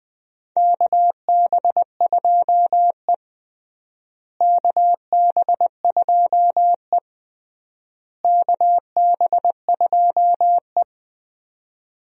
Morse code, heard as a keyed tone, about 21 wpm, 700 Hz. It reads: KB2E KB2E KB2E
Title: USA Call Signs - 4 Characters - Repeated 3x - 20wpm